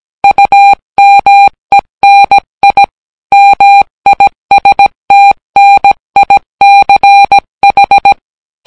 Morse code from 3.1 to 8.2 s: MISTNICH